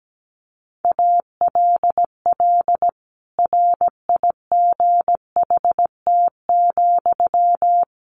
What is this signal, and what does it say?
Morse code, heard as a keyed tone, about 17 wpm, 700 Hz. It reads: ALL RIGHT,
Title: Sentences from Top 1100 Words 17wpm